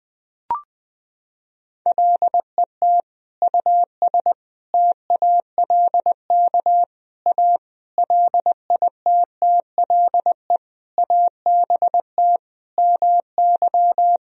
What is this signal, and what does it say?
Morse code, heard as a keyed tone, about 20 wpm, 700 Hz. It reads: LET US TALK A LITTLE ABT MY
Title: Sentences from Top 400 Words and Abbreviations 20wpm